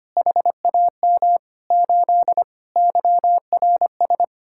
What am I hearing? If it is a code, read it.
Morse code, heard as a keyed tone, about 32 wpm, 700 Hz. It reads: HAM 8 YRS